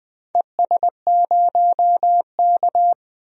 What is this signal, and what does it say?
Morse code, heard as a keyed tone, about 21 wpm, 700 Hz. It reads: ES0K
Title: DXCC-100 Call Signs - 4 Characters - 20wpm